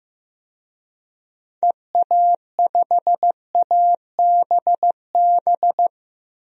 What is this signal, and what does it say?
Morse code, heard as a keyed tone, about 15 wpm, 700 Hz. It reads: EA5ABB
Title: DXCC-100 Call Signs - 6 Characters - 15wpm